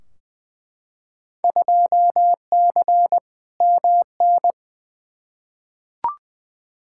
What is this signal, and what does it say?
Morse code, heard as a keyed tone, about 21 wpm, 700 Hz. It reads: 2C MN
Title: Field Day Exchanges Encore - 20wpm